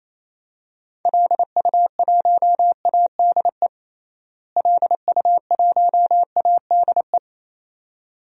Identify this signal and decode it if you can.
Morse code, heard as a keyed tone, about 28 wpm, 700 Hz. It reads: LU1ADE LU1ADE
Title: DXCC-100 Call Signs - 6 Characters - Repeated 2x - 28wpm